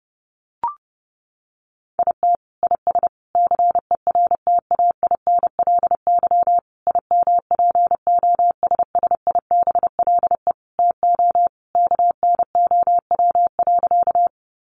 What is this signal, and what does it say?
Morse code, heard as a keyed tone, about 28 wpm, 700 Hz. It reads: IT IS CERTAINLY IMPOSSIBLE TO KNOW.
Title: Sentences from Top 1100 Words 30wpm